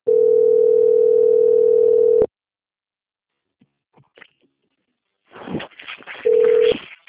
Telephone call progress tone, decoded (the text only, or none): ringback tone